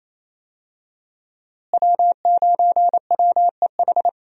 Morse code, heard as a keyed tone, about 28 wpm, 700 Hz. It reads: W9WEH
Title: USA Call Signs - 5 Characters - Repeated 2x - 28wpm